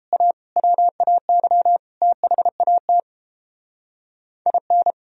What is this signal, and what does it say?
Morse code, heard as a keyed tone, about 29 wpm, 700 Hz. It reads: A WAY THAT IN